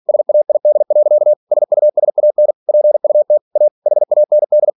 Morse code, heard as a keyed tone, about 46 wpm, 600 Hz. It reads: SAID. SUSAN PUT A HAND